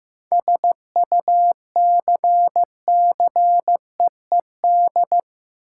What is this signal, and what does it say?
Morse code, heard as a keyed tone, about 15 wpm, 700 Hz. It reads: SUCCEED
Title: Seven Letter Words - 15wpm